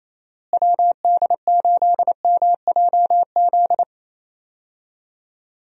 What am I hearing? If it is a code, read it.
Morse code, heard as a keyed tone, about 28 wpm, 700 Hz. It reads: WD8MJZ
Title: USA Call Signs - 6 Characters - Repeated 3x - 28wpm